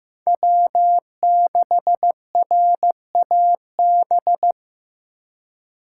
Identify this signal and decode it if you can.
Morse code, heard as a keyed tone, about 18 wpm, 700 Hz. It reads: W6RAB
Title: Speed Racing - USA Call Signs 15wpm